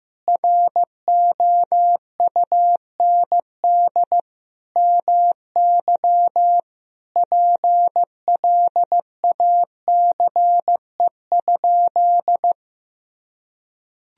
Morse code, heard as a keyed tone, about 15 wpm, 700 Hz. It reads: ROUND MY PLACE?
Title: Sentences from Top 200 Words and Abbreviations 15wpm